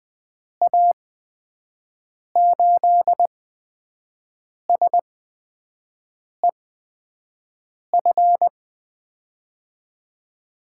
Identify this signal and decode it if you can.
Morse code, heard as a keyed tone, about 20 wpm, spaced as for 5 wpm, 700 Hz. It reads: A8SEF
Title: USA Call Signs - 8x Character Spacing - 20wpm